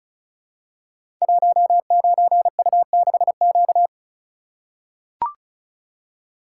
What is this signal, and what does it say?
Morse code, heard as a keyed tone, about 36 wpm, 700 Hz. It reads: 19U6Q
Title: Code Groups - 5 Letter:Numbers - 35wpm